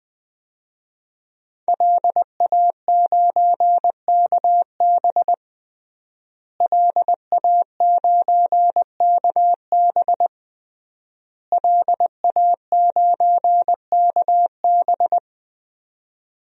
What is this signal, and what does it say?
Morse code, heard as a keyed tone, about 20 wpm, 700 Hz. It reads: LA9KB LA9KB LA9KB
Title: DXCC-100 Call Signs - 5 Characters - Repeated 3x - 20wpm